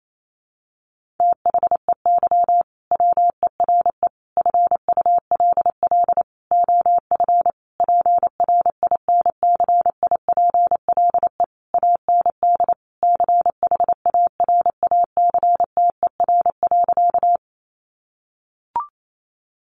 Morse code, heard as a keyed tone, about 28 wpm, 700 Hz. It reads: THEY WERE FULL OF PRINCIPLE AND CHARACTER.
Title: Sentences from Top 1400 Words 28wpm